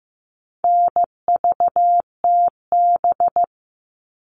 Morse code, heard as a keyed tone, about 15 wpm, 700 Hz. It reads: NVTB
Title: All Cut Numbers - 4-Digit 15wpm